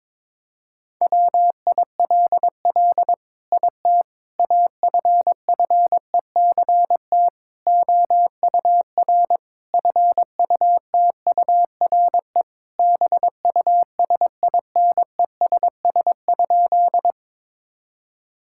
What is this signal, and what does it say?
Morse code, heard as a keyed tone, about 22 wpm, 700 Hz. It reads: WILL IT AFFECT OUR FUTURE BUSINESS?
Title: Sentences from Top 1000 Words - Review Part 6 - 22wpm